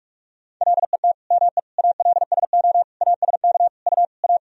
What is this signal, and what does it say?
Morse code, heard as a keyed tone, about 46 wpm, 700 Hz. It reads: LET ME ALSO ASK U A
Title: Sentences from Top 400 Words and Abbreviations 45wpm